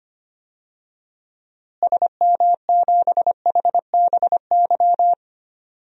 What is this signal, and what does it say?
Morse code, heard as a keyed tone, about 25 wpm, 700 Hz. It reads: SM7HBY
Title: DXCC-100 Call Signs - 6 Characters - Repeated 3x - 25wpm